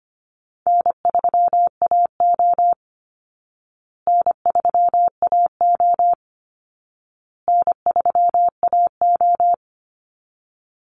Morse code, heard as a keyed tone, about 25 wpm, 700 Hz. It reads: N3AO N3AO N3AO
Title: Easy Call Signs Repeated 3x - 25wpm